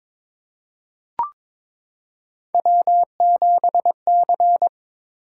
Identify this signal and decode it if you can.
Morse code, heard as a keyed tone, about 22 wpm, 700 Hz. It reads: W7C
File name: practice / 22wpm / Three Letter-Number Combinations - 22wpm